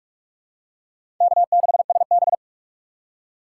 Morse code, heard as a keyed tone, about 46 wpm, 700 Hz. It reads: K6SB
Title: USA Call Signs - 4 Characters - 2x Character Spacing - 45wpm